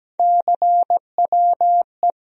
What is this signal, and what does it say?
Morse code, heard as a keyed tone, about 17 wpm, 700 Hz. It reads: CWE
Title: USA Call Sign Postfixes - 3 Letters 17wpm